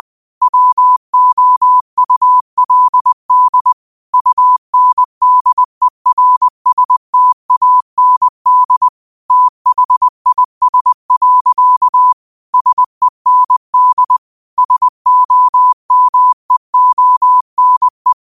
Morse code, heard as a keyed tone, about 20 wpm, 1000 Hz. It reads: WOULD UNDERSTAND THIS. SEND SOMEONE
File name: qotdfiles / 20wpm